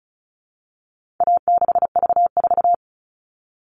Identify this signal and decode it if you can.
Morse code, heard as a keyed tone, about 34 wpm, 700 Hz. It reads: A6V4
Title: All Cut Numbers - 4-Digit 35wpm